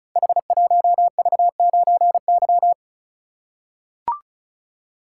Morse code, heard as a keyed tone, about 36 wpm, 700 Hz. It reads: H1V9Y
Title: Code Groups - 5 Letter:Numbers - 35wpm